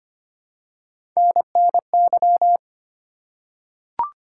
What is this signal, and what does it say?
Morse code, heard as a keyed tone, about 25 wpm, 700 Hz. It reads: NNY
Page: morse-code-ninja-practice-25wpm